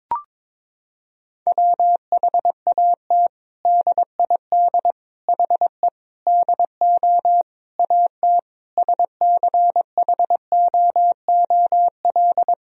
Morse code, heard as a keyed tone, about 22 wpm, 700 Hz. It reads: WHAT DID HE DO AT SCHOOL
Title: Sentences from Top 1000 Words - Review Part 1 - 22wpm